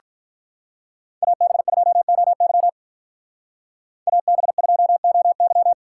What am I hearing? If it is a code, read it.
Morse code, heard as a keyed tone, about 45 wpm, 700 Hz. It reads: AB2OY AB2OY